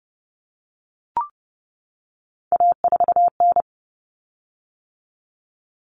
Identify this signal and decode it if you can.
Morse code, heard as a keyed tone, about 30 wpm, 700 Hz. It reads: A4N
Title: Common Cut Numbers - 3-Digit 30wpm